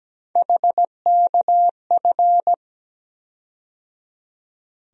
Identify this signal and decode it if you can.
Morse code, heard as a keyed tone, about 17 wpm, 700 Hz. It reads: HKF
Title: USA Call Sign Postfixes - 3 Letters 17wpm